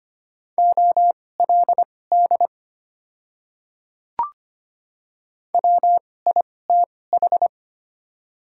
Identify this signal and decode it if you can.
Morse code, heard as a keyed tone, about 25 wpm, spaced as for 18 wpm, 700 Hz. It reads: OLD WITH
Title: 2x Character Spacing - Top 300 Words - 25wpm